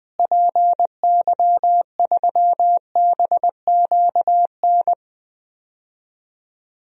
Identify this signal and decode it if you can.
Morse code, heard as a keyed tone, about 20 wpm, 700 Hz. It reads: PY3BQN
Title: DXCC-100 Call Signs - 6 Characters - Repeated 2x - 20wpm